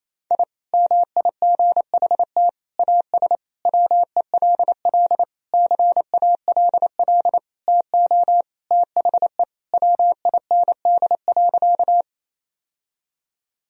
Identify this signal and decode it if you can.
Morse code, heard as a keyed tone, about 28 wpm, 700 Hz. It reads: I MIGHT AS WELL CALL TO THE WIND.
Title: Sentences from Top 1000 Words - Review Part 2 - 28wpm